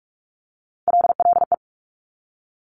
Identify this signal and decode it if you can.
Morse code, heard as a keyed tone, about 46 wpm, 700 Hz. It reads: LLE
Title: Top Three Letter Combinations in Words 45wpm